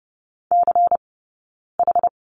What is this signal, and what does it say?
Morse code, heard as a keyed tone, about 30 wpm, spaced as for 9 wpm, 690 Hz. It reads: CH
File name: Lesson 17d - Letter u - Two Characters 30wpm